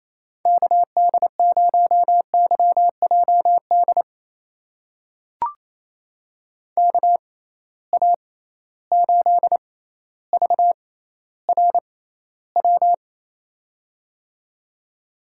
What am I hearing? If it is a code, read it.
Morse code, heard as a keyed tone, about 28 wpm, spaced as for 14 wpm, 700 Hz. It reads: KD0YJD KA8VRW
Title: USA Call Signs - 6x Character Spacing - 28wpm